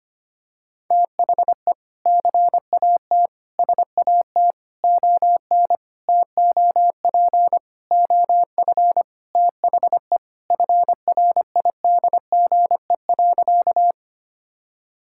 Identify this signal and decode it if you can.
Morse code, heard as a keyed tone, about 25 wpm, 700 Hz. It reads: THE CAT SAT ON TOP OF THE FRIDGE.
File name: practice / 25wpm / Sentences with 4-Word Phrases - 25wpm